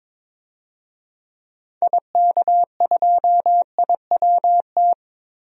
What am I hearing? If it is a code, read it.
Morse code, heard as a keyed tone, about 21 wpm, 700 Hz. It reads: IK2IWT
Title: DXCC-100 Call Signs - 6 Characters - Repeated 3x - 22wpm